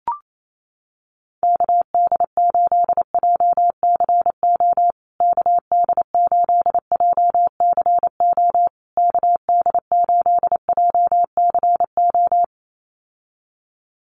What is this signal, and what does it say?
Morse code, heard as a keyed tone, about 28 wpm, 700 Hz. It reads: KD8JCO KD8JCO KD8JCO
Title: USA Call Signs Repeated 3x 28wpm